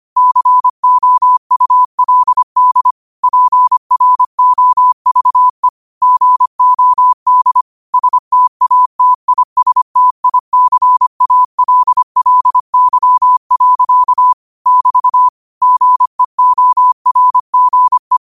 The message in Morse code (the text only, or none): COULD PROVE GOD STATISTICALLY. <BT> GEORGE